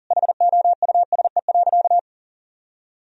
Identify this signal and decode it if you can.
Morse code, heard as a keyed tone, about 37 wpm, 700 Hz. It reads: HOUSE.
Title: Sentences from Top 1000 Words - Review Part 1 - 40wpm